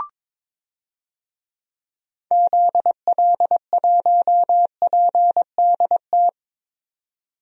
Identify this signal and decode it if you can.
Morse code, heard as a keyed tone, about 22 wpm, 700 Hz. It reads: ZL1PDT
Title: DXCC-100 Call Signs - 6 Characters - 22wpm